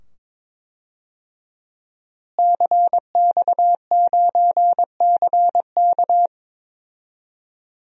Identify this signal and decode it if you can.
Morse code, heard as a keyed tone, about 22 wpm, 700 Hz. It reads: CX9CK
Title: DXCC-100 Call Signs - 5 Characters - Repeated 2x - 22wpm